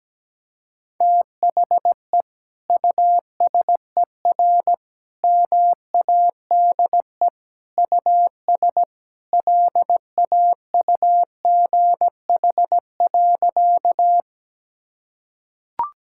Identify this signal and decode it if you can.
Morse code, heard as a keyed tone, about 17 wpm, 700 Hz. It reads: THE USER MADE US LAUGH.
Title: Sentences from Top 1000 Words - Review Part 5 - 17wpm